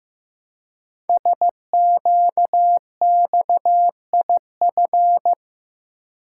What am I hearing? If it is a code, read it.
Morse code, heard as a keyed tone, about 15 wpm, 700 Hz. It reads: SQXIF
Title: Code Groups - 5 Letter:Numbers - 15wpm